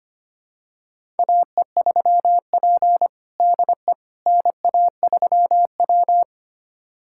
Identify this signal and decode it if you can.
Morse code, heard as a keyed tone, about 25 wpm, 700 Hz. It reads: AE3P DE NA3W
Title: USA Callsign DE Callsign - 4 Characters - 25wpm